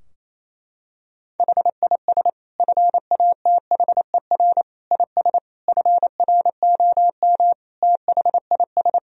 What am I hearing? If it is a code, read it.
Morse code, heard as a keyed tone, about 27 wpm, 700 Hz. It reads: HIS FATHER IS FROM THIS
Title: Sentences from Top 1000 Words - Review Part 2 - 28wpm